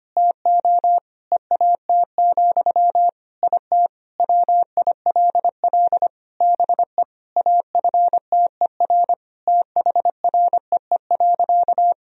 Morse code, heard as a keyed tone, about 25 wpm, 700 Hz. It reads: TO EAT, IT WILL BE AFTER THREE.
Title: Sentences from Top 1000 Words - Review Part 1 - 25wpm